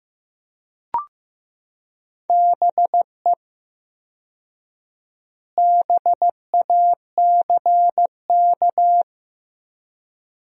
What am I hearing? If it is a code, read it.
Morse code, heard as a keyed tone, about 16 wpm, 700 Hz. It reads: BE BACK
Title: 4x Word-Spacing - Sentences from Top 500 Words - 15wpm